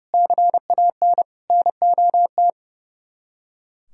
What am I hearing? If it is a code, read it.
Morse code, heard as a keyed tone, about 37 wpm, 700 Hz. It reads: CAN NOT
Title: Speed-Racing - Sets of 4 Words 30wpm